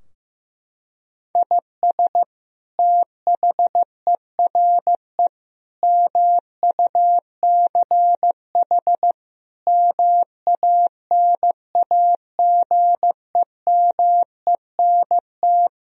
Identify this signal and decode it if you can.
Morse code, heard as a keyed tone, about 16 wpm, 700 Hz. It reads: IS THERE MUCH MANAGEMENT
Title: Sentences from Top 1000 Words - Review Part 4 - 15wpm